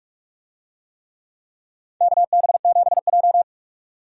Morse code, heard as a keyed tone, about 46 wpm, 700 Hz. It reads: KB7J
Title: USA Call Signs - 4 Characters - 45wpm